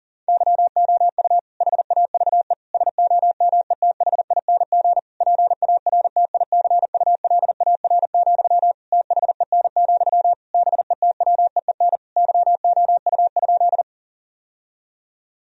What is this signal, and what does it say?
Morse code, heard as a keyed tone, about 40 wpm, 700 Hz. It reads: YOU HAVE SOMETHING PARTICULAR, THEN, BETWEEN YOU?
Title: Sentences from Top 1000 Words - Review Part 4 - 40wpm